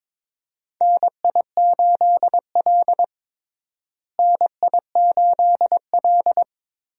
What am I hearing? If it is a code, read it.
Morse code, heard as a keyed tone, about 22 wpm, 700 Hz. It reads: NI8L NI8L
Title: USA Call Signs - 4 Characters - Repeated 3x - 22wpm